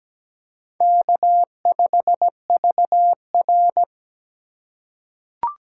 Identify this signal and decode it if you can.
Morse code, heard as a keyed tone, about 17 wpm, 700 Hz. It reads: K5VR
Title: K1USN SST QSO Practice - 17wpm